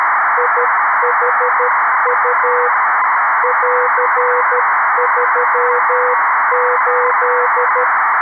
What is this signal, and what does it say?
Morse code, heard as a keyed tone, about 13 wpm, 500 Hz, with noise. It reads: IHU <AR>38
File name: arissat / recordings / ARISSat 11Sept2011 1354utc 60frames